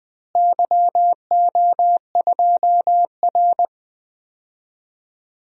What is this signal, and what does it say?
Morse code, heard as a keyed tone, about 20 wpm, 700 Hz. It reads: YO2R
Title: DXCC-100 Call Signs - 4 Characters - Repeated 3x - 20wpm